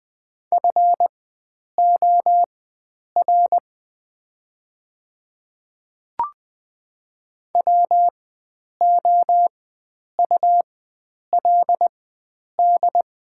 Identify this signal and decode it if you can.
Morse code, heard as a keyed tone, about 20 wpm, spaced as for 9 wpm, 700 Hz. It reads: FOR WOULD
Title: 4x Character Spacing - Top 100 Words - 20wpm